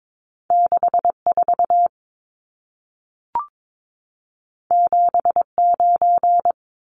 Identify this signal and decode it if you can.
Morse code, heard as a keyed tone, about 22 wpm, 700 Hz. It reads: 64 79
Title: Numbers - 2-Digit 22wpm